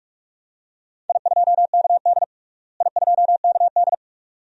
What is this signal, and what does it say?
Morse code, heard as a keyed tone, about 46 wpm, 700 Hz. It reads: I2KD I2KD
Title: DXCC-100 Call Signs - 4 Characters - Repeated 2x - 45wpm